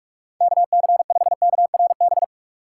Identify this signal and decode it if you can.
Morse code, heard as a keyed tone, about 47 wpm, 700 Hz. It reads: KC5KRB
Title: USA Call Signs - 6 Characters - 3x Character Spacing - 45wpm